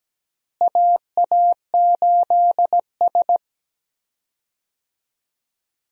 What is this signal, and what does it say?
Morse code, heard as a keyed tone, about 17 wpm, 700 Hz. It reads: AA8S
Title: USA Call Signs - 4 Characters - Repeated 3x - 17wpm